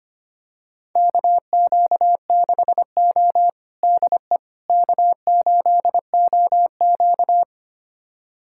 Morse code, heard as a keyed tone, about 25 wpm, 700 Hz. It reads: KQ6O DE K8OQ
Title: USA Callsign DE Callsign - 4 Characters - 25wpm